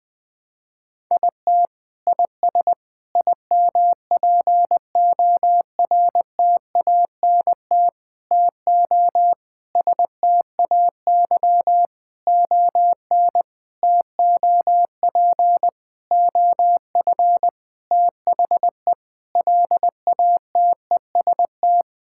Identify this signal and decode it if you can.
Morse code, heard as a keyed tone, about 20 wpm, 700 Hz. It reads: IT IS IMPORTANT TO STAY ON TOP OF THE LATEST
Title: Sentences with 6-Word Phrases - 20wpm